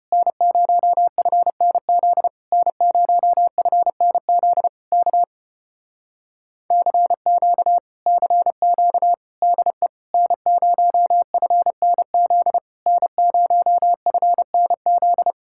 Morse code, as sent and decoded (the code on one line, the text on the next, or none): -. ----- ..-. -. --.. / -. ----- ..-. -. --.. / -.- / -.-. --.- / -.-. --.- / -.. / -. ----- ..-. -. --.. / -. ----- ..-. -. --..
N0FNZ N0FNZ K CQ CQ DE N0FNZ N0FNZ